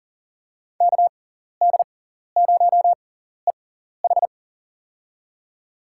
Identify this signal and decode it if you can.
Morse code, heard as a keyed tone, about 40 wpm, spaced as for 14 wpm, 700 Hz. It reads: KD0EH
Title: USA Call Signs - 5 Characters - 6x Character Spacing - 40wpm